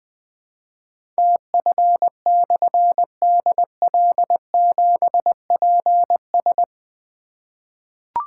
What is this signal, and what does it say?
Morse code, heard as a keyed tone, about 21 wpm, 700 Hz. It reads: TF/DL7PS
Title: Difficult Call Signs 20wpm